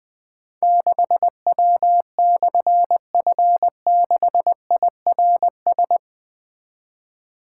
Morse code, heard as a keyed tone, about 20 wpm, 700 Hz. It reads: 6W/F6IRS
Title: Difficult Call Signs 20wpm